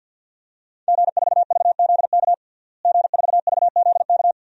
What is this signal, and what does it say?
Morse code, heard as a keyed tone, about 47 wpm, 700 Hz. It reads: G4VZK G4VZK
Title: DXCC-100 Call Signs - 5 Characters - Repeated 2x - 50wpm